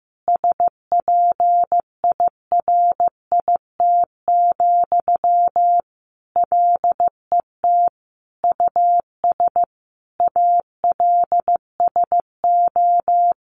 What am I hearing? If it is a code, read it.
Morse code, heard as a keyed tone, about 15 wpm, 700 Hz. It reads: SPIRIT, LET US ALSO
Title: Sentences from Top 1400 Words 15wpm